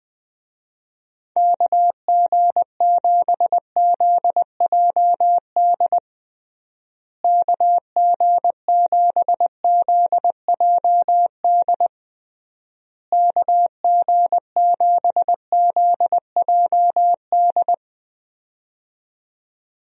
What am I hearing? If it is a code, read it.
Morse code, heard as a keyed tone, about 20 wpm, 700 Hz. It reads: KG7ZJD KG7ZJD KG7ZJD
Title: USA Call Signs - 6 Characters - Repeated 3x - 20wpm